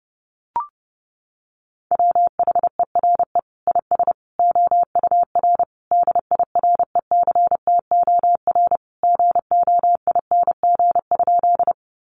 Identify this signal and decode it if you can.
Morse code, heard as a keyed tone, about 28 wpm, 700 Hz. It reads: WHERE IS OUR DIRECTOR GOING?
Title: Sentences from Top 1300 Words 30wpm